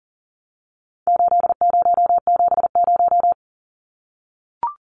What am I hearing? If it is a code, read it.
Morse code, heard as a keyed tone, about 40 wpm, 700 Hz. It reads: 8070
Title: Numbers - 4-Digit 40wpm